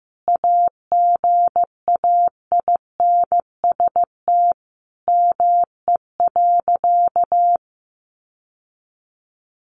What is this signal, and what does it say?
Morse code, heard as a keyed tone, about 15 wpm, 700 Hz. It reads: AGAINST ME.
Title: Sentences from Top 1100 Words 15wpm